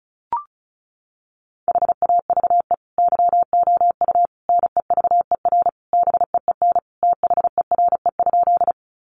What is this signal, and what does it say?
Morse code, heard as a keyed tone, about 34 wpm, 700 Hz. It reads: HAVE YOU NEVER BEEN THERE?
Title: Sentences from Top 300 Words 35wpm